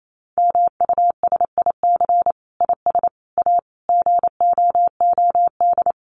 Morse code, heard as a keyed tone, about 28 wpm, 700 Hz. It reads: MUSIC IS A GOOD
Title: Sentences from Top 300 Words 28wpm